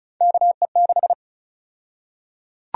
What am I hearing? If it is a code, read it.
Morse code, heard as a keyed tone, about 34 wpm, 700 Hz. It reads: KE6
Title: USA Call Sign Prefixes - 35wpm